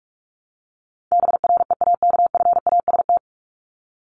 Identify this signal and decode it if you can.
Morse code, heard as a keyed tone, about 44 wpm, 700 Hz. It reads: BREAKFAST
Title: Top 1251-1500 Words 45wpm